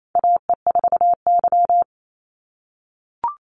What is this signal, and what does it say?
Morse code, heard as a keyed tone, about 28 wpm, 700 Hz. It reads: AE4Y
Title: NA SOTA Chaser Callsigns 28wpm